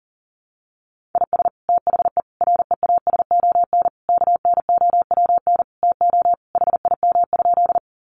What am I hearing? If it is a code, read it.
Morse code, heard as a keyed tone, about 40 wpm, 700 Hz. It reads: IS THE REASON KNOWN TO HIM?
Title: Sentences from Top 700 Words 40wpm